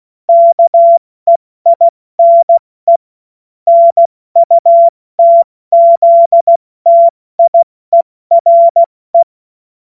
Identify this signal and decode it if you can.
Morse code, heard as a keyed tone, about 16 wpm, 680 Hz. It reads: KEINE NUTZTIERE
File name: Morsekurs / Witze / Witz 07. März 2023 70 F80